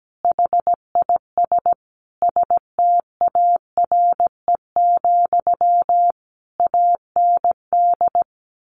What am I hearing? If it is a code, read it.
Morse code, heard as a keyed tone, about 17 wpm, 700 Hz. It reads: HIS STARE, AND
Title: A Sound Of Thunder 17wpm